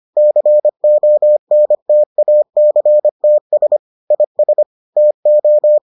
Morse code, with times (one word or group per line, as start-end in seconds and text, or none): 0.2-3.8 CONTACTS
4.1-4.6 IS
5.0-5.8 TO